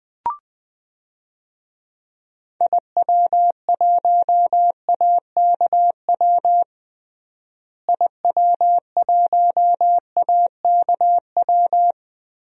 Morse code, heard as a keyed tone, about 21 wpm, 700 Hz. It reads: IW1AKW IW1AKW
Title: DXCC-100 Call Signs - 6 Characters - Repeated 3x - 20wpm